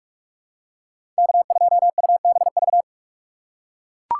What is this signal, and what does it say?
Morse code, heard as a keyed tone, about 42 wpm, 700 Hz. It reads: K2UBV
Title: USA Call Signs - 5 Characters - 3x Character Spacing - 45wpm